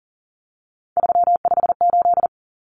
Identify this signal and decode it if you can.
Morse code, heard as a keyed tone, about 40 wpm, 700 Hz. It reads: 358
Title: Numbers - 3-Digit 40wpm